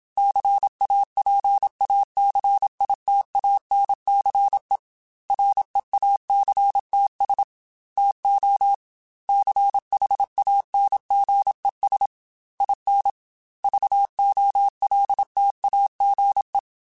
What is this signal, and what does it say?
Morse code, heard as a keyed tone, about 27 wpm, 800 Hz. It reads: CAPACITANCE REACTS TO CHANGES IN VOLTAGE
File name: Technical Subjects / Technical Subjects-mp3s / Reactance @25